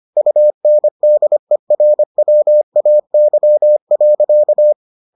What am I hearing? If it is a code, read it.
Morse code, heard as a keyed tone, about 25 wpm, 600 Hz. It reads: UNDERWAY.